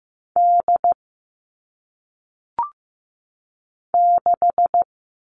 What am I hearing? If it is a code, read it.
Morse code, heard as a keyed tone, about 15 wpm, 700 Hz. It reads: D 6
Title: All Cut Numbers - 1-Digit 15wpm